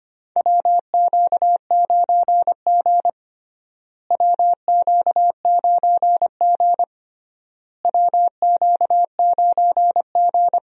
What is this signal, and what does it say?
Morse code, heard as a keyed tone, about 25 wpm, 700 Hz. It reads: WQ9G WQ9G WQ9G
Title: USA Call Signs - 4 Characters - Repeated 3x - 25wpm